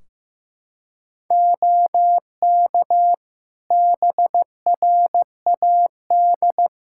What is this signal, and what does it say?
Morse code, heard as a keyed tone, about 15 wpm, 700 Hz. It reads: OK BRAD
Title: K1USN SST QSO Practice 15wpm